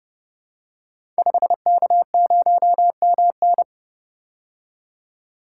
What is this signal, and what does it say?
Morse code, heard as a keyed tone, about 30 wpm, 700 Hz. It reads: 5K0MN
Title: Code Groups - 5 Letter:Numbers - 30wpm